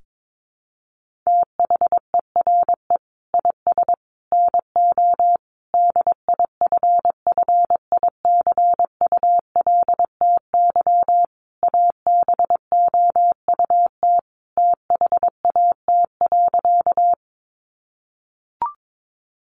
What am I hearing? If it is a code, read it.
Morse code, heard as a keyed tone, about 22 wpm, 700 Hz. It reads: THERE IS NO DIFFICULTY ABOUT THAT.
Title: Sentences from Top 1500 Words 22wpm